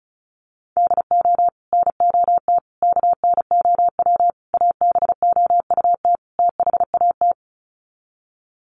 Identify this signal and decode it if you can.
Morse code, heard as a keyed tone, about 34 wpm, 700 Hz. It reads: DO NOT KNOW ABOUT THAT
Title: Sets of 5 Words Encore2 35wpm